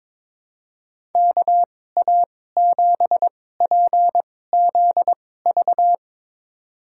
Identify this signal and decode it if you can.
Morse code, heard as a keyed tone, about 21 wpm, spaced as for 16 wpm, 700 Hz. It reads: KA7PZV
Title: USA Call Signs - 6 Characters - 2x Character Spacing - 22wpm